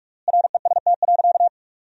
Morse code, heard as a keyed tone, about 42 wpm, 700 Hz. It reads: REST.
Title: 3x Word-Spacing - Sentences from Top 500 Words - 45wpm